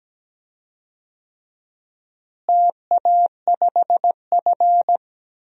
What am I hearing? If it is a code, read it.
Morse code, heard as a keyed tone, about 17 wpm, 700 Hz. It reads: TA5F
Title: DXCC-100 Call Signs - 4 Characters - Repeated 2x - 17wpm